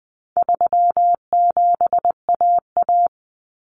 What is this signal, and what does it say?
Morse code, heard as a keyed tone, about 20 wpm, 700 Hz. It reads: 37AA
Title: Common Cut Numbers - 4-Digit 20wpm